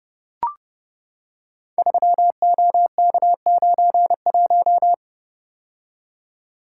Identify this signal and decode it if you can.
Morse code, heard as a keyed tone, about 29 wpm, 700 Hz. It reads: 3OK91
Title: Code Groups - 5 Letter:Numbers - 30wpm